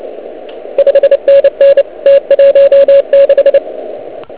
Morse code, heard as a keyed tone, about 29 wpm, 560 Hz, with radio noise. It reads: 5NN T16